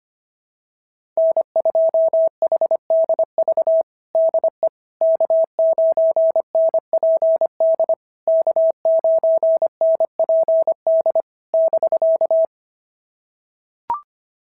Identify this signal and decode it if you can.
Morse code, heard as a keyed tone, about 25 wpm, 660 Hz. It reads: N2HDV DE K9NPD K9NPD <BK>
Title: SOTA QSO Practice 25wpm